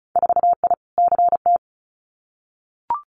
Morse code, heard as a keyed tone, about 36 wpm, 700 Hz. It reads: 4I CT
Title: Winter Field Day Exchanges Encore 35wpm